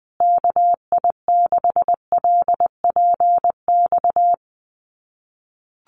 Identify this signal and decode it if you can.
Morse code, heard as a keyed tone, about 20 wpm, 700 Hz. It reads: KI6LPX
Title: USA Call Signs Repeated 3x 20wpm